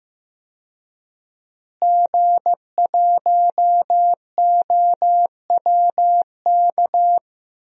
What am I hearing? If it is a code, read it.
Morse code, heard as a keyed tone, about 15 wpm, 700 Hz. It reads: G1OWK
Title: DXCC-100 Call Signs - 5 Characters - Repeated 3x - 15wpm